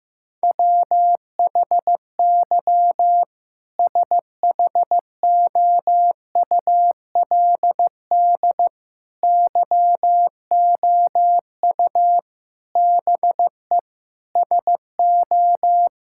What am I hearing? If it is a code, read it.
Morse code, heard as a keyed tone, about 15 wpm, 700 Hz. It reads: WHY SHOULD YOU BE SO
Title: Sentences from Top 1000 Words - Review Part 6 - 15wpm